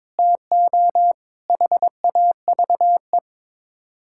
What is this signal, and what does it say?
Morse code, heard as a keyed tone, about 22 wpm, 700 Hz. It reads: TO HAVE
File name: practice / 22wpm / Build A Phrase - 5-Word Phrases - 22wpm